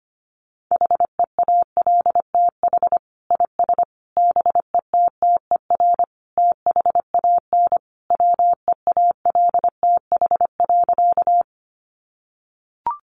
Morse code, heard as a keyed tone, about 25 wpm, 700 Hz. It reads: HEALTH IS BETTER THAN WEALTH.
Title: Sentences from Top 1200 Words 25wpm